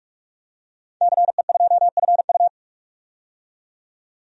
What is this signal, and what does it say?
Morse code, heard as a keyed tone, about 45 wpm, 690 Hz. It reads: CE2FU
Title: DXCC-100 Call Signs - 5 Characters - 45wpm